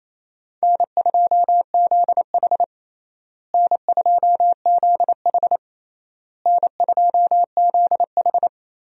Morse code, heard as a keyed tone, about 28 wpm, 700 Hz. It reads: N2ZH N2ZH N2ZH
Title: USA Call Signs - 4 Characters - Repeated 3x - 28wpm